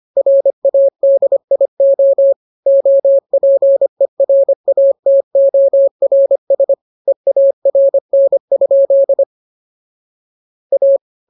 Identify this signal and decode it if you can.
Morse code, heard as a keyed tone, about 25 wpm, 550 Hz. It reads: RADIO OPERATORS EARN? A
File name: jokes / joke 2 25 0000